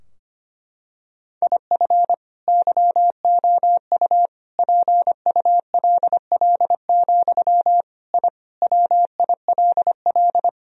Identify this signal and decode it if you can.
Morse code, heard as a keyed tone, about 25 wpm, 700 Hz. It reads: IF YOU PULL, I WILL
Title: Sentences from Top 1000 Words - Review Part 2 - 25wpm